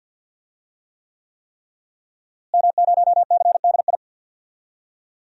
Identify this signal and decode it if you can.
Morse code, heard as a keyed tone, about 52 wpm, 700 Hz. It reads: M0CDI
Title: DXCC-100 Call Signs - 5 Characters - Repeated 3x - 50wpm